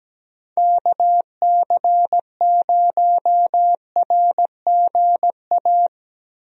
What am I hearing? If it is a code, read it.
Morse code, heard as a keyed tone, about 17 wpm, 700 Hz. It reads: KC0RGA
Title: USA Call Signs - 6 Characters - Repeated 2x - 17wpm